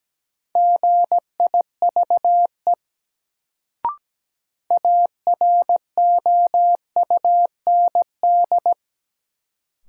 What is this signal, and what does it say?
Morse code, heard as a keyed tone, about 17 wpm, 700 Hz. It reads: GIVE AROUND
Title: Sets of 2 Words - New Word Drill - 17wpm